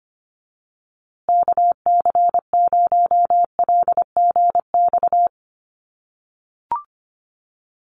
Morse code, heard as a keyed tone, about 25 wpm, 700 Hz. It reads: KC0LGX
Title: USA Call Signs Repeated 2x 25wpm